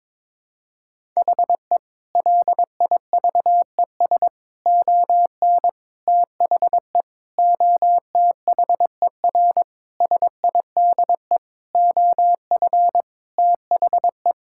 Morse code, heard as a keyed tone, about 21 wpm, 700 Hz. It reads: HE LIVES ON THE OTHER SIDE OF THE
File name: practice / 22wpm / Sentences with 4-Word Phrases - 22wpm